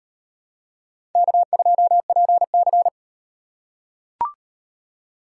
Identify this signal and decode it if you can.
Morse code, heard as a keyed tone, about 35 wpm, 700 Hz. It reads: K2PC